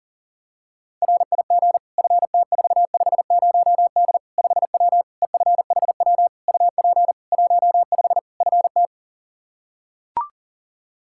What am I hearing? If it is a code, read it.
Morse code, heard as a keyed tone, about 37 wpm, 700 Hz. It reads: RIG FT450D 5W EFHW UP 15 FT
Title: Practice QSO - Exchange 2 - Info - 40wpm